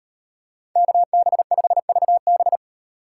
Morse code, heard as a keyed tone, about 33 wpm, 700 Hz. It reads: KB5VB